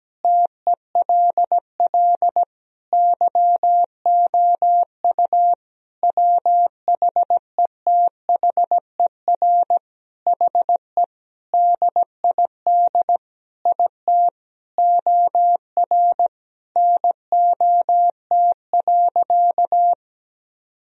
Morse code, heard as a keyed tone, about 17 wpm, 700 Hz. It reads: TELL YOU WHETHER HE DID IT OR NOT.
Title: Sentences from Top 1000 Words - Review Part 3 - 17wpm